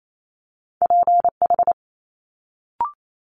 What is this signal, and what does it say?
Morse code, heard as a keyed tone, about 28 wpm, 700 Hz. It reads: PH
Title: Multi-letter Phonemes Quick Fire 28wpm